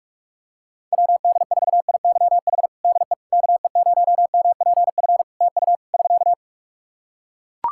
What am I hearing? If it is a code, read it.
Morse code, heard as a keyed tone, about 44 wpm, 700 Hz. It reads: WD4IYH DE KE0MPF TU <SK>
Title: Practice QSO - Exchange 4 - Ending - 45wpm